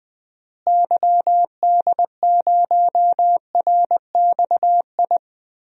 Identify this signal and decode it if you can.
Morse code, heard as a keyed tone, about 20 wpm, 700 Hz. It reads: YD0RXI
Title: DXCC-100 Call Signs - 6 Characters - Repeated 2x - 20wpm